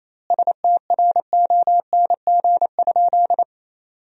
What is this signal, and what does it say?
Morse code, heard as a keyed tone, about 28 wpm, 700 Hz. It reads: STRONG?
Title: 4x Word-Spacing - Sentences from Top 500 Words - 28wpm